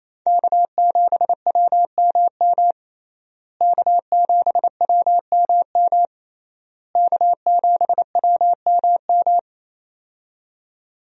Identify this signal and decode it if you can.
Morse code, heard as a keyed tone, about 28 wpm, 700 Hz. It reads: K7WMM K7WMM K7WMM
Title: USA Call Signs - 5 Characters - Repeated 3x - 28wpm